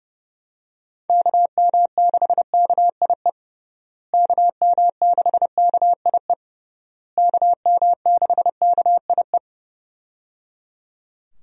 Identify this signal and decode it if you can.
Morse code, heard as a keyed tone, about 29 wpm, 700 Hz. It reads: KM6KIE KM6KIE KM6KIE
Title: USA Call Signs - 6 Characters - Repeated 3x - 30wpm